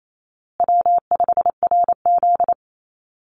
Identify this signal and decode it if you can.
Morse code, heard as a keyed tone, about 28 wpm, 700 Hz. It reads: W5RZ
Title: Easy Call Signs Repeated 3x - 28wpm